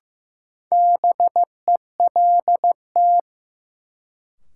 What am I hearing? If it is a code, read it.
Morse code, heard as a keyed tone, about 15 wpm, 700 Hz. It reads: BELT
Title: Sentences from Top 1400 Words - New Word Drill Alt - 15wpm